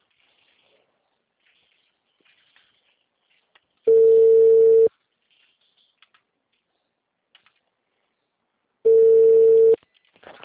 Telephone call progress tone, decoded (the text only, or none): ringback tone